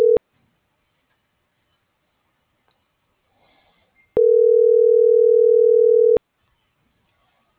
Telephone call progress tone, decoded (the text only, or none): ringback tone